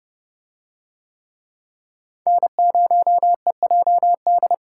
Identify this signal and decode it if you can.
Morse code, heard as a keyed tone, about 28 wpm, 700 Hz. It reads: N0EJD